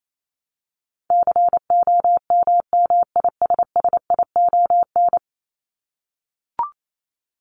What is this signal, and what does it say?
Morse code, heard as a keyed tone, about 28 wpm, 700 Hz. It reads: COMMISSION
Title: Top 1251-1500 Words 28wpm